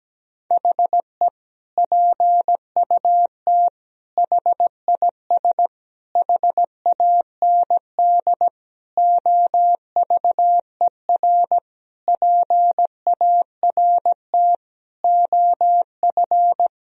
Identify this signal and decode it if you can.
Morse code, heard as a keyed tone, about 17 wpm, 700 Hz. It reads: HE PUT HIS HAND OVER PART OF
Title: Sentences from Top 1000 Words - Review Part 2 - 17wpm